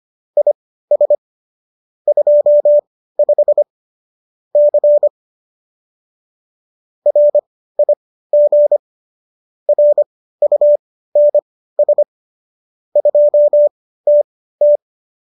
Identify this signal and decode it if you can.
Morse code, heard as a keyed tone, about 25 wpm, spaced as for 15 wpm, 590 Hz. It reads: IS 25 C RIG RUNS 2TT